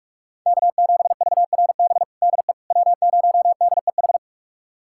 Morse code, heard as a keyed tone, about 44 wpm, 700 Hz. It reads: K7VRB DE W0DEH
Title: USA Callsign DE Callsign - 5 Characters - 45wpm